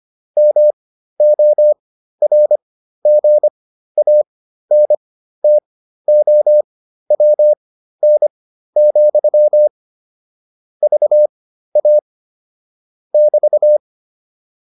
Morse code, heard as a keyed tone, about 25 wpm, spaced as for 13 wpm, 600 Hz. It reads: MORGANTOWN, VA <BT>